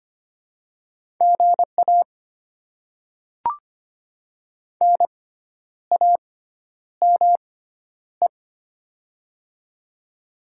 Morse code, heard as a keyed tone, about 25 wpm, spaced as for 10 wpm, 700 Hz. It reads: GA NAME
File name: Top 20 QSO Elements - 6x Character Spacing - 25wpm